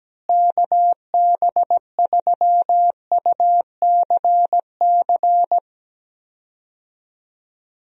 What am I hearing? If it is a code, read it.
Morse code, heard as a keyed tone, about 17 wpm, 700 Hz. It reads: KB3UCC
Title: USA Call Signs - 6 Characters - Repeated 3x - 17wpm